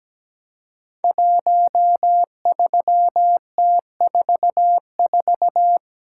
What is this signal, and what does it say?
Morse code, heard as a keyed tone, about 17 wpm, 700 Hz. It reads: 13T44
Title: Code Groups - 5 Letter:Numbers - 17wpm